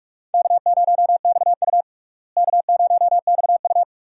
Morse code, heard as a keyed tone, about 46 wpm, 700 Hz. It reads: K0XU K0XU
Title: USA Call Signs - 4 Characters - Repeated 2x - 45wpm